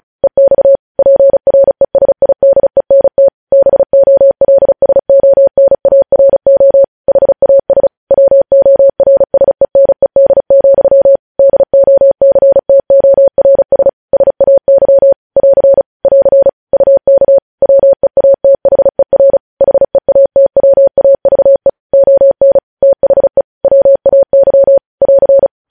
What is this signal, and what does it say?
Morse code, heard as a keyed tone, about 36 wpm, 550 Hz. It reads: EX PRESIDENT BOLSONARO HAS WORSENED, DOCTORS SAY <AR> <AR> UK WEATHER HEATWAVE ON THE WAY <AR>
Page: News Headlines in Morse Code at 35 WPM